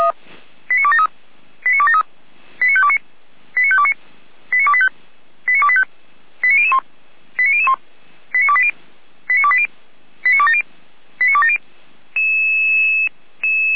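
Five-tone selective call (ZVEI), heard as quick runs of five tones, tones of about 70 ms. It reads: 89273 89273 89529 89529 89285 89285 89001 89001 89280 89280 89270 89270
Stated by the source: ZVEI five-tone selective call received by SDR, from the Feuerwehr Orlamünde fire brigade audio server.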